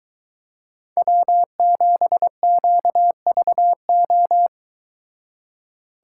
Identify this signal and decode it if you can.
Morse code, heard as a keyed tone, about 22 wpm, 700 Hz. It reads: W7QVO